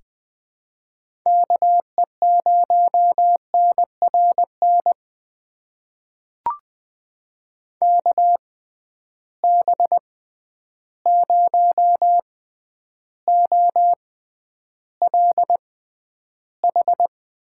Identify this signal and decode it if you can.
Morse code, heard as a keyed tone, about 20 wpm, spaced as for 10 wpm, 700 Hz. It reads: KE0NRN KB0OLH
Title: USA Call Signs - 6 Characters - 6x Character Spacing - 20wpm